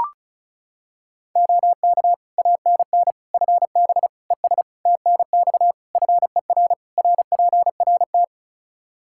Morse code, heard as a keyed tone, about 34 wpm, 700 Hz. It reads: OK ANN FB ES TNX FER RPRT
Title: K1USN SST QSO Practice 35wpm